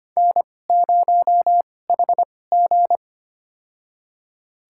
Morse code, heard as a keyed tone, about 25 wpm, spaced as for 18 wpm, 700 Hz. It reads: N0HG
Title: USA Call Signs - 4 Characters - 2x Character Spacing - 25wpm